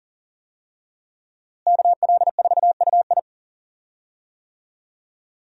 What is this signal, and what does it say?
Morse code, heard as a keyed tone, about 40 wpm, 700 Hz. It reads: KL4UI
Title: USA Call Signs - 5 Characters - Repeated 3x - 40wpm